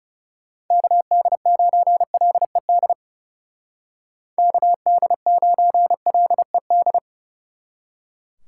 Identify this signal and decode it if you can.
Morse code, heard as a keyed tone, about 37 wpm, 700 Hz. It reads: KD9LED KD9LED
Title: Speed Racing - USA Call Signs 30wpm